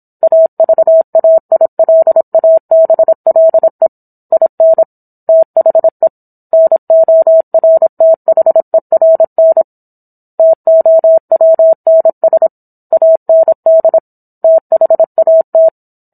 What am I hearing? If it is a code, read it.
Morse code, heard as a keyed tone, about 27 wpm, 650 Hz. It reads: AVAILABLE IN THE NORTHERN TOWNS AND THAT